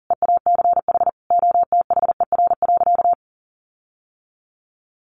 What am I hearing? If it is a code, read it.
Morse code, heard as a keyed tone, about 40 wpm, 700 Hz. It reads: EACH OTHER.
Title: Sentences from Top 1300 Words 40wpm